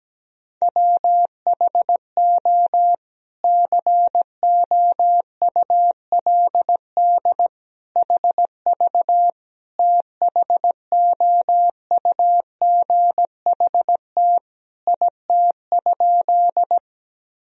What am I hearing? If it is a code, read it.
Morse code, heard as a keyed tone, about 17 wpm, 700 Hz. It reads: WHO COULD HV THOUGHT IT?
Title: Sentences from Top 300 Words and Abbreviations 17wpm